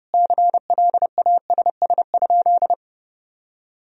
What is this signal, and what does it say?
Morse code, heard as a keyed tone, about 32 wpm, 700 Hz. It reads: CLASS?
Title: Sentences from Top 1000 Words - Review Part 2 - 30wpm